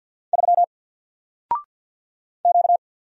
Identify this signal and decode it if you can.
Morse code, heard as a keyed tone, about 48 wpm, 700 Hz. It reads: 3 Q
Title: Single Letter-Number Alt 50wpm